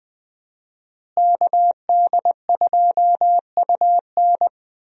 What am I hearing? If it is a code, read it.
Morse code, heard as a keyed tone, about 20 wpm, 700 Hz. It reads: KD2UN
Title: USA Call Signs - 5 Characters - Repeated 3x - 20wpm